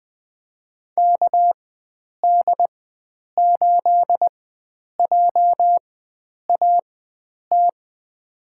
Morse code, heard as a keyed tone, about 20 wpm, spaced as for 9 wpm, 700 Hz. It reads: KD8JAT
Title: USA Call Signs - 6 Characters - 4x Character Spacing - 20wpm